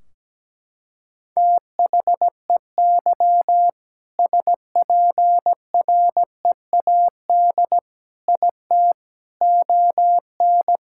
Morse code, heard as a keyed tone, about 17 wpm, 700 Hz. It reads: THEY SPREAD IT ON
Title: Sentences from Top 1600 Words 17wpm